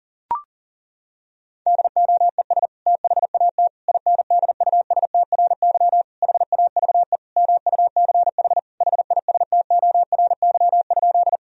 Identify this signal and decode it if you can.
Morse code, heard as a keyed tone, about 40 wpm, 700 Hz. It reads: DOES THAT INDUSTRY HAVE MUCH HISTORY?
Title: Sentences from Top 1000 Words - Review Part 4 - 40wpm